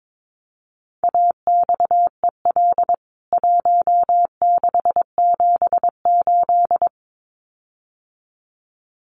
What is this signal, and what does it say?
Morse code, heard as a keyed tone, about 22 wpm, 700 Hz. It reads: AXEL 1678
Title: CWT Exchange 22wpm